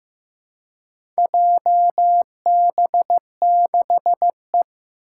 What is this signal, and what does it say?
Morse code, heard as a keyed tone, about 15 wpm, 700 Hz. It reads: JB6E
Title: Four Challenging Letter-Number Combinations 15wpm